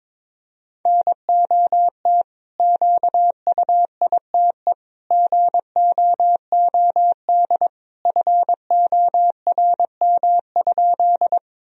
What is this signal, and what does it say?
Morse code, heard as a keyed tone, about 22 wpm, 700 Hz. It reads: NOT QUITE GOOD FORM?
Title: Sentences from Top 1000 Words - Review Part 1 - 22wpm